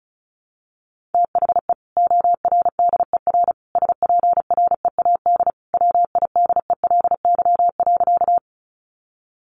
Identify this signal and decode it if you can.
Morse code, heard as a keyed tone, about 34 wpm, 700 Hz. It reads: THE ORDER SPREAD WIDELY.
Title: Sentences from Top 1200 Words 35wpm